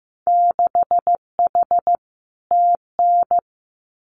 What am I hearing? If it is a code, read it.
Morse code, heard as a keyed tone, about 15 wpm, 700 Hz. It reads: 6H TN
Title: Winter Field Day Exchanges Encore 15wpm